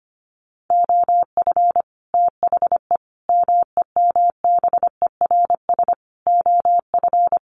Morse code, heard as a keyed tone, about 25 wpm, 700 Hz. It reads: OF THE MEMBERS OF
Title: Sets of 5 Words Encore2 25wpm